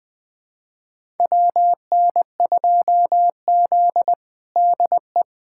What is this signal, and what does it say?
Morse code, heard as a keyed tone, about 21 wpm, 700 Hz. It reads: WN2Z DE
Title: USA Callsign DE Callsign - 4 Characters - 20wpm